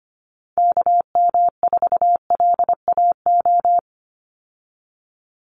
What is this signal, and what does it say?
Morse code, heard as a keyed tone, about 25 wpm, 700 Hz. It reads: KM4LAO
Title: Call Signs Repeated 3x - 25wpm